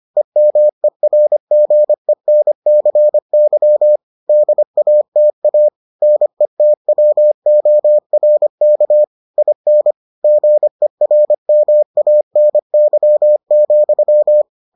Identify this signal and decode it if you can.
Morse code, heard as a keyed tone, about 25 wpm, 600 Hz. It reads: EMERGENCY DATA NETWORK IN GERMANY,